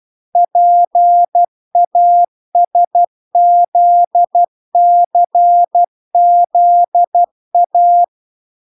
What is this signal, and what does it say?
Morse code, heard as a keyed tone, about 12 wpm, 700 Hz. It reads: PASZCZA